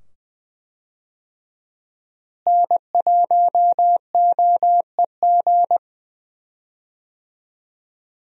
Morse code, heard as a keyed tone, about 21 wpm, 700 Hz. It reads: N1OEG
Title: USA Call Signs - 5 Characters - Repeated 2x - 20wpm